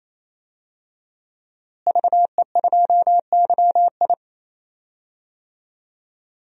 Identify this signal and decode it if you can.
Morse code, heard as a keyed tone, about 27 wpm, 700 Hz. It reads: VE2YI
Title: DXCC-100 Call Signs - 5 Characters - Repeated 3x - 28wpm